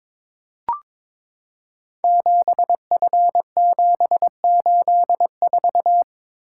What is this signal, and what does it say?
Morse code, heard as a keyed tone, about 22 wpm, 700 Hz. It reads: 7F784
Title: Code Groups - 5 Letter:Numbers - 22wpm